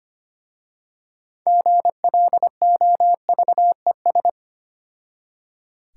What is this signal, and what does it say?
Morse code, heard as a keyed tone, about 25 wpm, 700 Hz. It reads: GLOVES
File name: practice / 25wpm / Difficult Words with Hints - 25wpm